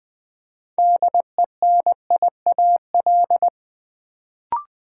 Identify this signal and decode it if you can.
Morse code, heard as a keyed tone, about 20 wpm, 700 Hz. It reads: DENIAL
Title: Build A Word - 3 Letters At A Time - 6-Letter Words - 20wpm